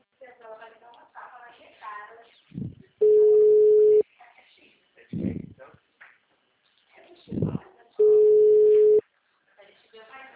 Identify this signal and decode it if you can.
Ringback tone (Europe, 425 Hz), with speech.